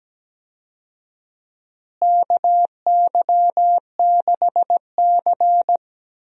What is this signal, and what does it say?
Morse code, heard as a keyed tone, about 17 wpm, 700 Hz. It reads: KY6C
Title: USA Call Signs - 4 Characters - Repeated 2x - 17wpm